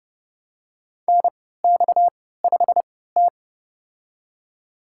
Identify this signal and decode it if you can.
Morse code, heard as a keyed tone, about 30 wpm, spaced as for 17 wpm, 700 Hz. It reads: NX5T
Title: USA Call Signs - 4 Characters - 3x Character Spacing - 30wpm